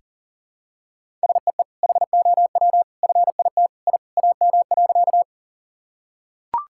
Morse code, heard as a keyed tone, about 37 wpm, 700 Hz. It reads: SEE HOW FIT I AM.
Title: Sentences from Top 1000 Words - Review Part 5 - 40wpm